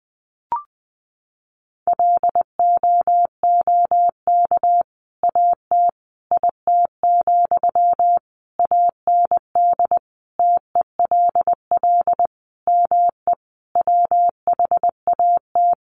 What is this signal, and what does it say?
Morse code, heard as a keyed tone, about 20 wpm, 700 Hz. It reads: LOOK AT IT, AND TELL ME WHAT
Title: Sentences from Top 200 Words 20wpm